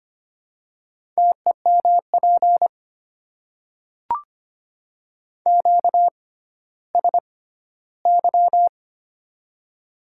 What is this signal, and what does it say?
Morse code, heard as a keyed tone, about 25 wpm, spaced as for 14 wpm, 690 Hz. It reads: TEMP QSY
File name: Top 90 QSO Elements - 6x Character Spacing - 25wpm